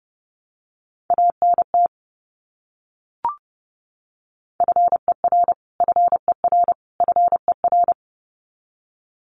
Morse code, heard as a keyed tone, about 30 wpm, 700 Hz. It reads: ANT FER FER FER
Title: Abbreviations Repeated 3x - 30wpm